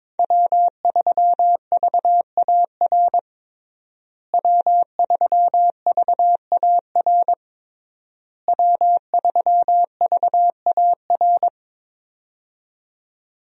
Morse code, heard as a keyed tone, about 21 wpm, 700 Hz. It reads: W3VAR W3VAR W3VAR
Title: USA Call Signs - 5 Characters - Repeated 3x - 22wpm